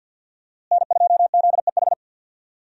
Morse code, heard as a keyed tone, about 53 wpm, 700 Hz. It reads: N2ZEH
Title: USA Call Signs - 5 Characters - Repeated 2x - 50wpm